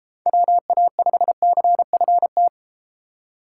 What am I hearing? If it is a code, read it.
Morse code, heard as a keyed tone, about 31 wpm, 700 Hz. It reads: WA5CFT